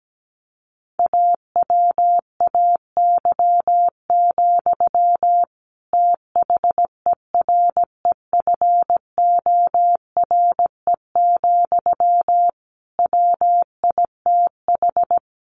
Morse code, heard as a keyed tone, about 17 wpm, 700 Hz. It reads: AWAY, THEREFORE, WITH
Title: Sentences from Top 1500 Words 17wpm